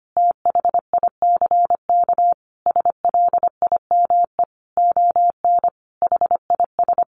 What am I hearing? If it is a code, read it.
Morse code, heard as a keyed tone, about 25 wpm, 700 Hz. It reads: THICK SLIME ON HIS
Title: A Sound Of Thunder 25wpm